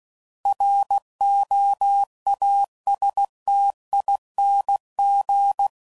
Morse code, heard as a keyed tone, about 16 wpm, 800 Hz. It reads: ROASTING
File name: Suffixes / Special INGs / Sngl Ings